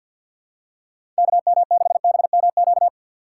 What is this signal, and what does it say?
Morse code, heard as a keyed tone, about 54 wpm, 700 Hz. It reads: KM6BMQ
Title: USA Call Signs - 6 Characters - 6x Character Spacing - 50wpm